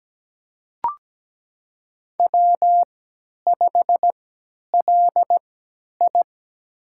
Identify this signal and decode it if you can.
Morse code, heard as a keyed tone, about 17 wpm, spaced as for 10 wpm, 700 Hz. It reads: W5LI
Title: USA Call Signs - 4 Characters - 3x Character Spacing - 17wpm